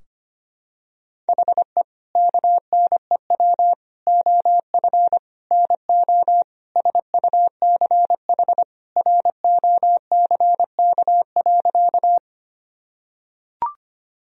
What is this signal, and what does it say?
Morse code, heard as a keyed tone, about 25 wpm, 700 Hz. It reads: HE KNEW OF NO SUCH ROCK.
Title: Sentences from Top 1000 Words - Review Part 2 - 25wpm